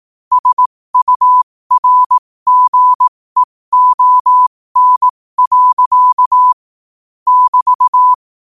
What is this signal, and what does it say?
Morse code, heard as a keyed tone, about 18 wpm, spaced as for 15 wpm, 1000 Hz. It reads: SURGEON. <BT>
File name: qotdfiles / 15wpm